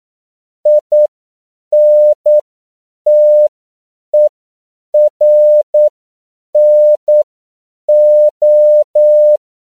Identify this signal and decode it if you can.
Morse code, heard as a keyed tone, about 9 wpm, spaced as for 7 wpm, 600 Hz. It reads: INTERNO
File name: CW / 10PPM